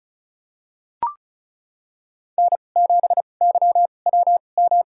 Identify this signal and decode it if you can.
Morse code, heard as a keyed tone, about 36 wpm, spaced as for 25 wpm, 700 Hz. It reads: N7YWM
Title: USA Call Signs - 5 Characters - 2x Character Spacing - 35wpm